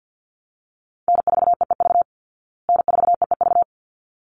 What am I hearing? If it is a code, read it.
Morse code, heard as a keyed tone, about 50 wpm, 700 Hz. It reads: N4EEV N4EEV
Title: Call Signs Repeated 3x - 50wpm